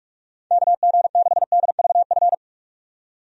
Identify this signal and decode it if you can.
Morse code, heard as a keyed tone, about 46 wpm, 700 Hz. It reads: KG6DVF
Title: USA Call Signs - 6 Characters - 45wpm